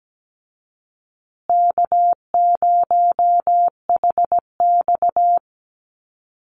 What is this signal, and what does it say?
Morse code, heard as a keyed tone, about 17 wpm, 700 Hz. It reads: K0HX
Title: Easy Call Signs Repeated 2x - 17wpm